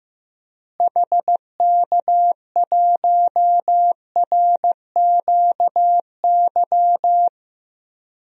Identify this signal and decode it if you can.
Morse code, heard as a keyed tone, about 15 wpm, 700 Hz. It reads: HK1RQY
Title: DXCC-100 Call Signs - 6 Characters - Repeated 2x - 15wpm